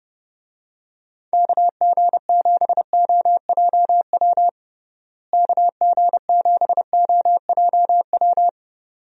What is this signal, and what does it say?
Morse code, heard as a keyed tone, about 30 wpm, 700 Hz. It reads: KG7OJW KG7OJW
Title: USA Call Signs - 6 Characters - Repeated 2x - 30wpm